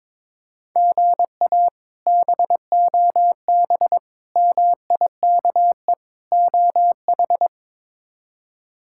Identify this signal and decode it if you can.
Morse code, heard as a keyed tone, about 22 wpm, 700 Hz. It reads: GA BOB MIKE OH
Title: K1USN SST QSO Practice - 22wpm